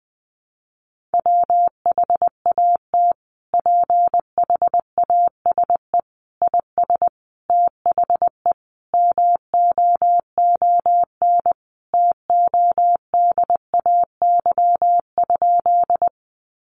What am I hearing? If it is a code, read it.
Morse code, heard as a keyed tone, about 19 wpm, 700 Hz. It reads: WHAT PHASE IS THE MOON TODAY?
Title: Sentences from Top 1200 Words 20wpm